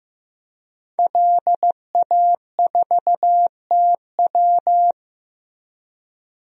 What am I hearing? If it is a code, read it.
Morse code, heard as a keyed tone, about 15 wpm, 700 Hz. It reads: LA4TW
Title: DXCC-100 Call Signs - 5 Characters - 15wpm